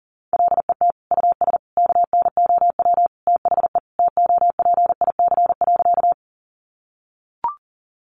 Morse code, heard as a keyed tone, about 40 wpm, 700 Hz. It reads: LET US KNOW THE TOPIC.
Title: Sentences from Top 1200 Words 40wpm